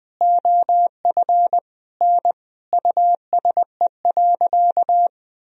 Morse code, heard as a keyed tone, about 21 wpm, 700 Hz. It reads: OF N USE.
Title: Sentences from Top 100 Words and Abbreviations 20wpm